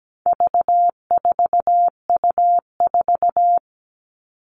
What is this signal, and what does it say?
Morse code, heard as a keyed tone, about 17 wpm, 700 Hz. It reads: V4U4
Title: All Cut Numbers - 4-Digit 17wpm